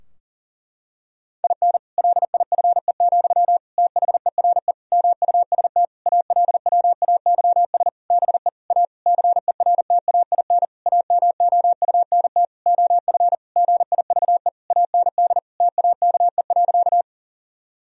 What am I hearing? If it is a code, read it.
Morse code, heard as a keyed tone, about 40 wpm, 700 Hz. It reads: IN LIFE, THERE MUST ALWAYS BE A CERTAIN AMOUNT OF GIVE AND TAKE.
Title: Sentences from Top 1000 Words - Review Part 4 - 40wpm